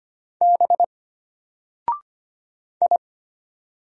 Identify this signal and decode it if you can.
Morse code, heard as a keyed tone, about 25 wpm, 700 Hz. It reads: B I
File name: Single-Letter-Number-25wpm